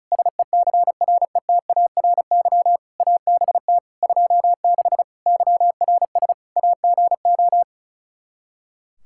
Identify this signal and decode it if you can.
Morse code, heard as a keyed tone, about 36 wpm, 700 Hz. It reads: SECRETARY ABT 26 YRS AGO
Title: K1USN SST QSO Practice 35wpm